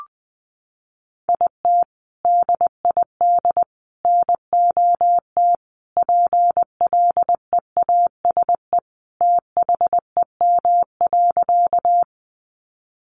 Morse code, heard as a keyed tone, about 20 wpm, 700 Hz. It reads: IT DID NOT PLEASE THEM.
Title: Sentences from Top 1500 Words 20wpm